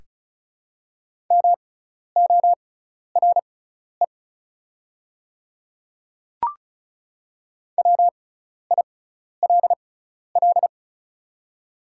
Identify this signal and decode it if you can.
Morse code, heard as a keyed tone, about 36 wpm, spaced as for 12 wpm, 700 Hz. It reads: MORE WILL
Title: 6x Character Spacing - Top 100 Words - 35wpm